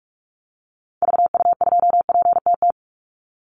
Morse code, heard as a keyed tone, about 44 wpm, 700 Hz. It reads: VU2PTT
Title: Call Signs - 45wpm